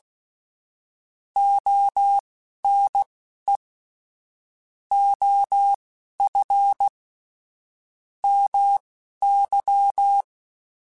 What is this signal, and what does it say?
Morse code, heard as a keyed tone, about 16 wpm, spaced as for 11 wpm, 800 Hz. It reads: ONE OF MY
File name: QSOs / QSOs-mp3s / WD0FBT 1@15